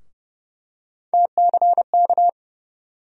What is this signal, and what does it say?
Morse code, heard as a keyed tone, about 30 wpm, 700 Hz. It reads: TCK
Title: USA Call Sign Postfixes - 3 Letters 30wpm